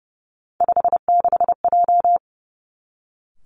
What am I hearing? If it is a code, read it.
Morse code, heard as a keyed tone, about 30 wpm, 700 Hz. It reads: 56J
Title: Three Challenging Letter-Number Combinations 30wpm